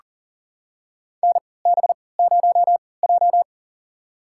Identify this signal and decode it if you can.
Morse code, heard as a keyed tone, about 40 wpm, spaced as for 23 wpm, 700 Hz. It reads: NB0J
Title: USA Call Signs - 4 Characters - 3x Character Spacing - 40wpm